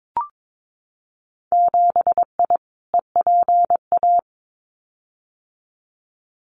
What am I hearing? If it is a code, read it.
Morse code, heard as a keyed tone, about 22 wpm, 700 Hz. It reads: 7I EPA
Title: Winter Field Day Exchanges Encore 22wpm